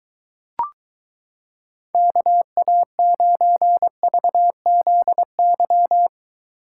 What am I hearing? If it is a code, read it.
Morse code, heard as a keyed tone, about 23 wpm, 700 Hz. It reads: KA9VZY